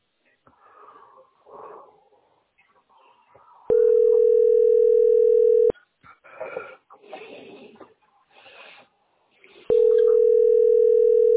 Ringback tone (North America), 440 + 480 Hz.